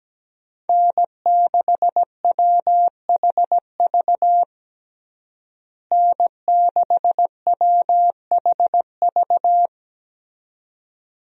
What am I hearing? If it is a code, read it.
Morse code, heard as a keyed tone, about 17 wpm, 700 Hz. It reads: N6WHV N6WHV
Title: USA Call Signs - 5 Characters - Repeated 2x - 17wpm